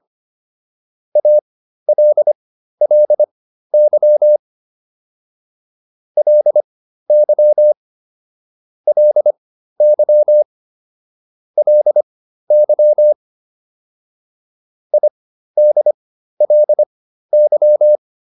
Morse code, heard as a keyed tone, about 25 wpm, spaced as for 13 wpm, 600 Hz. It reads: ALLY LY LY LY IDLY